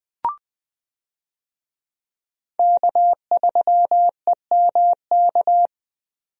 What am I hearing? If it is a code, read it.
Morse code, heard as a keyed tone, about 20 wpm, 700 Hz. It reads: K3EMK
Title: USA Call Signs - 5 Characters - 20wpm